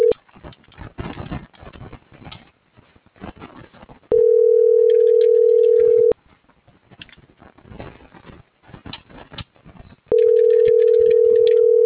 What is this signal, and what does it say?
Ringback tone (North America), 440 + 480 Hz.